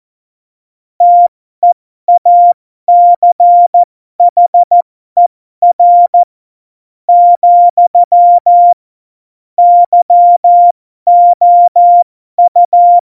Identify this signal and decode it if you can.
Morse code, heard as a keyed tone, about 14 wpm, spaced as for 11 wpm, 700 Hz. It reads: TEACHER, YOU
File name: Morsekurs / Witze